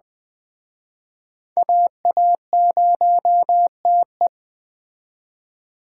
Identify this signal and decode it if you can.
Morse code, heard as a keyed tone, about 21 wpm, 700 Hz. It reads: AA0TE
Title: Speed Racing - USA 2x2 Call Signs 17wpm